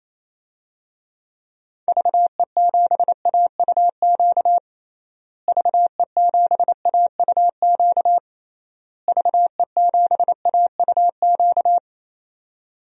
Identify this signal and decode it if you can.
Morse code, heard as a keyed tone, about 28 wpm, 700 Hz. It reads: VE7AUQ VE7AUQ VE7AUQ
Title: DXCC-100 Call Signs - 6 Characters - Repeated 3x - 28wpm